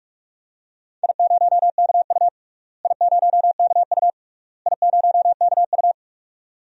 Morse code, heard as a keyed tone, about 43 wpm, 700 Hz. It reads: I0KU I0KU I0KU
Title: DXCC-100 Call Signs - 4 Characters - Repeated 3x - 45wpm